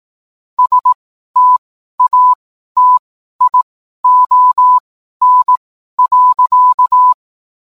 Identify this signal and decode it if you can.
Morse code, heard as a keyed tone, about 18 wpm, spaced as for 12 wpm, 1000 Hz. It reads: STATION.